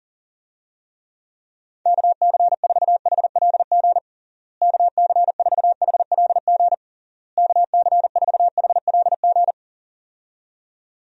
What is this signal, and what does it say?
Morse code, heard as a keyed tone, about 40 wpm, 700 Hz. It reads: KC4HLG KC4HLG KC4HLG
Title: USA Call Signs - 6 Characters - Repeated 3x - 40wpm